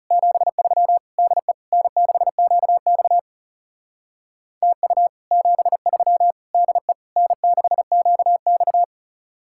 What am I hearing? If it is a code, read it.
Morse code, heard as a keyed tone, about 36 wpm, 700 Hz. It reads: 73 DE N6QX TU 73 DE N6QX
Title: Speed-Racing Common QSO Phrases Encore - 30wpm